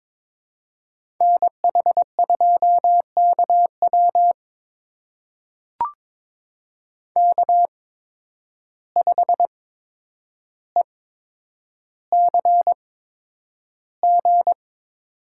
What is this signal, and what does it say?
Morse code, heard as a keyed tone, about 21 wpm, spaced as for 9 wpm, 700 Hz. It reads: NH2KW K5ECG
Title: USA Call Signs - 5 Characters - 8x Character Spacing - 22wpm